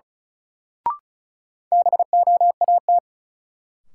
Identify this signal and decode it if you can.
Morse code, heard as a keyed tone, about 36 wpm, 700 Hz. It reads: BOAT
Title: Sentences from Top 500 Words - New Word Drill - 35wpm